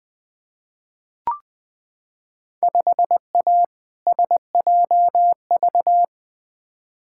Morse code, heard as a keyed tone, about 20 wpm, 700 Hz. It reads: 5A SJV
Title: Field Day Exchanges - 20wpm